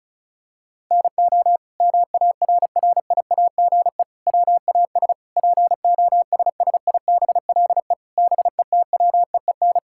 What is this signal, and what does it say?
Morse code, heard as a keyed tone, about 36 wpm, 700 Hz. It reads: NO MARRIAGE WAS POSSIBLE BETWEEN
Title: Sentences from Top 1000 Words - Review Part 5 - 35wpm